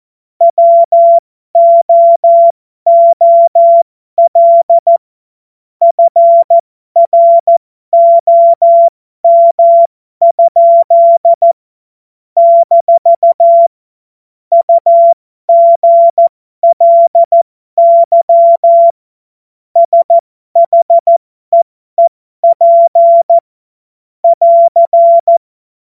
Morse code, heard as a keyed tone, about 14 wpm, spaced as for 12 wpm, 680 Hz. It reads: WOOL FROM? - UGLY SHEEP <AR>
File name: Morsekurs / Witze